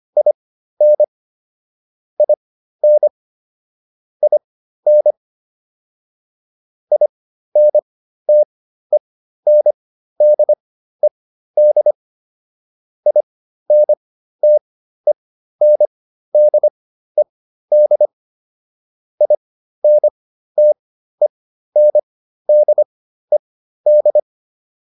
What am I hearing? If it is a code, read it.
Morse code, heard as a keyed tone, about 25 wpm, spaced as for 13 wpm, 600 Hz. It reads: IN IN IN INTENDED INTENDED INTENDED